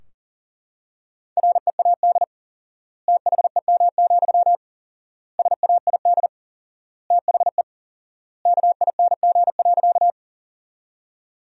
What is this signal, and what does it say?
Morse code, heard as a keyed tone, about 40 wpm, 700 Hz. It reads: READ THEM, SAID THE KING.
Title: 4x Word-Spacing - Sentences from Top 400 Words - 40wpm